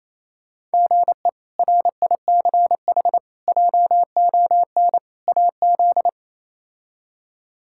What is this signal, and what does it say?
Morse code, heard as a keyed tone, about 28 wpm, 700 Hz. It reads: GE RICH JON AZ
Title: K1USN SST QSO Practice - 28wpm